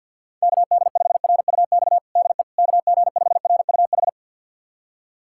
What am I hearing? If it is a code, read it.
Morse code, heard as a keyed tone, about 52 wpm, 690 Hz. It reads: KD5RUX DE KG5RUH
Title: USA Callsign DE Callsign - 6 Characters - 50wpm